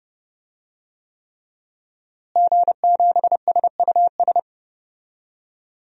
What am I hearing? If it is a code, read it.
Morse code, heard as a keyed tone, about 30 wpm, 700 Hz. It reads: G7SUS
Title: DXCC-100 Call Signs - 5 Characters - Repeated 3x - 30wpm